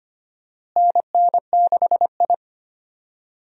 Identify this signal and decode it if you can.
Morse code, heard as a keyed tone, about 25 wpm, 700 Hz. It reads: NN6I
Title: USA Call Signs - 4 Characters - Repeated 3x - 25wpm